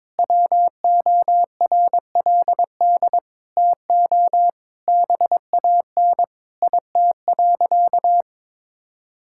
Morse code, heard as a keyed tone, about 21 wpm, 700 Hz. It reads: WORLD TO BAN IT.
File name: practice / 22wpm / Sentences with 8-Word Phrases - 22wpm